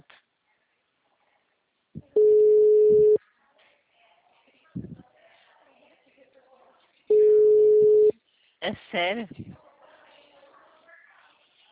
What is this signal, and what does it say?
Ringback tone (Europe, 425 Hz), with speech.